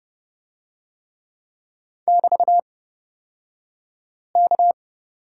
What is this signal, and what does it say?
Morse code, heard as a keyed tone, about 30 wpm, spaced as for 5 wpm, 700 Hz. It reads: <BT>K
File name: Lesson 36d - BT - Two Characters 30wpm